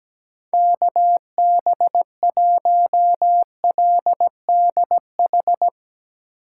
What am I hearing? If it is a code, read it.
Morse code, heard as a keyed tone, about 18 wpm, 700 Hz. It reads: KB1LDH
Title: Speed Racing - USA 2x3 Call Signs 17wpm